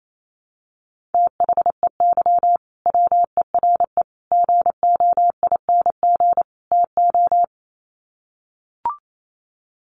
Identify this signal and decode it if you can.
Morse code, heard as a keyed tone, about 28 wpm, 700 Hz. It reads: THEY WERE GOING TO
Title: Sets of 4 Words 28wpm